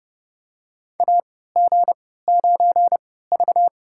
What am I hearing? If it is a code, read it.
Morse code, heard as a keyed tone, about 30 wpm, spaced as for 17 wpm, 700 Hz. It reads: AG9V
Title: USA Call Signs - 4 Characters - 3x Character Spacing - 30wpm